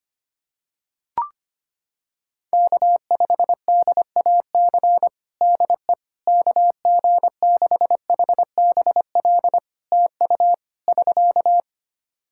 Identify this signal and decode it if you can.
Morse code, heard as a keyed tone, about 25 wpm, 700 Hz. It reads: K5DAC DE KG6HBL TU <SK>
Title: K1USN SST QSO Practice 25wpm